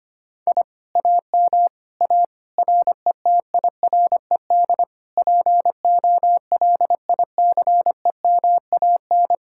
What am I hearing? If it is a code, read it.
Morse code, heard as a keyed tone, about 24 wpm, 700 Hz. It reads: I AM A RETIRED POLICEMAN
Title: Speed-Racing Common QSO Phrases - 25wpm